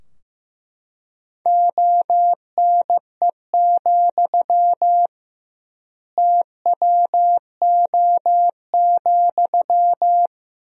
Morse code, heard as a keyed tone, about 15 wpm, 700 Hz. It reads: ONE, TWO,
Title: 2x Word-Spacing - Sentences from Top 200 Words - 15wpm